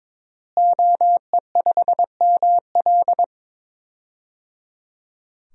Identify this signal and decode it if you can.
Morse code, heard as a keyed tone, about 22 wpm, 700 Hz. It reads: OE5ML
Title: DXCC-100 Call Signs - 5 Characters - Repeated 3x - 22wpm